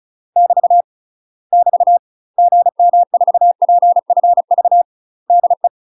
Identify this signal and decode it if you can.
Morse code, heard as a keyed tone, about 36 wpm, 700 Hz. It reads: <BT> <BT> GM4PFV DE
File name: UK MASTER QSOs TRAINING / 35WPM / UK MASTER QSOs 0014